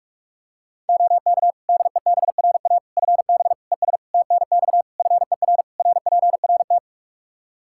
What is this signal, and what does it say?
Morse code, heard as a keyed tone, about 41 wpm, 700 Hz. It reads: OK DEBRA FB ES TNX FER RPRT
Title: Practice QSO - Exchange 2 - Info - 45wpm